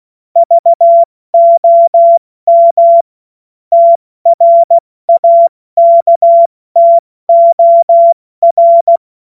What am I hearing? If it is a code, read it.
Morse code, heard as a keyed tone, about 16 wpm, spaced as for 14 wpm, 680 Hz. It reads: VOM TRAKTOR
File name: Morsekurs / Witze / Witz 12. Oktober 2021, 70 F80